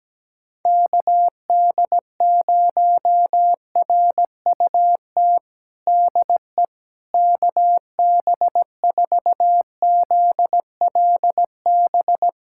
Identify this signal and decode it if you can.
Morse code, heard as a keyed tone, about 17 wpm, 700 Hz. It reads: KD0RUT DE KB4ZLB
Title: USA Callsign DE Callsign - 6 Characters - 17wpm